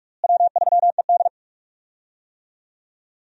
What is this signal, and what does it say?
Morse code, heard as a keyed tone, about 44 wpm, 700 Hz. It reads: W3ED
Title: USA Call Signs - 4 Characters - Repeated 2x - 45wpm